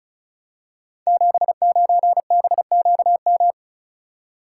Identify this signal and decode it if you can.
Morse code, heard as a keyed tone, about 36 wpm, 690 Hz. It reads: 79BQM
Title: Code Groups - 5 Letter:Numbers - 35wpm